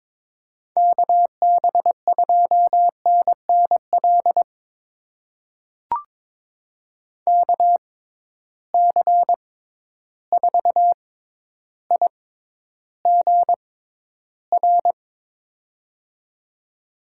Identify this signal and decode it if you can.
Morse code, heard as a keyed tone, about 22 wpm, spaced as for 11 wpm, 700 Hz. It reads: KB2NNL KC4IGR
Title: USA Call Signs - 6 Characters - 6x Character Spacing - 22wpm